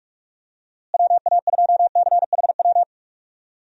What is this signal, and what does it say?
Morse code, heard as a keyed tone, about 42 wpm, 700 Hz. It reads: WA2CHW
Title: USA Call Signs - 6 Characters - Repeated 3x - 45wpm